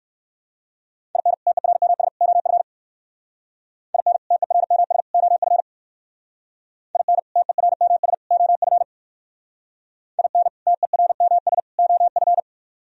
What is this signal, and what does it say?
Morse code, heard as a keyed tone, about 63 wpm, 700 Hz. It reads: IN TERMS OF IN TERMS OF IN TERMS OF IN TERMS OF